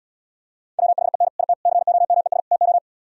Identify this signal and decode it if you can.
Morse code, heard as a keyed tone, about 77 wpm, 700 Hz. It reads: WHEN IT COMES TO